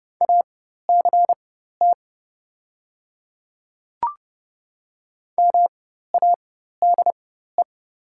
Morse code, heard as a keyed tone, about 30 wpm, spaced as for 14 wpm, 690 Hz. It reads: ACT MADE
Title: 4x Character Spacing - Top 300 Words - 30wpm